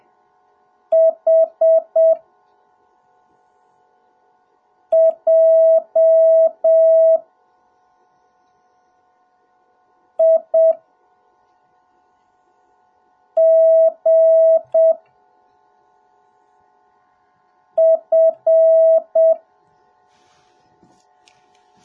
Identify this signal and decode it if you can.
Morse code, heard as a keyed tone, about 7 wpm, spaced as for 3 wpm, 650 Hz. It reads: HJIGF